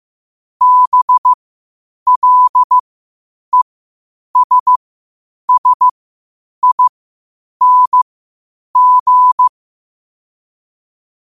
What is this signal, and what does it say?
Morse code, heard as a keyed tone, about 15 wpm, spaced as for 8 wpm, 1000 Hz. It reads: BLESSING
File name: qotdfiles / 8wpm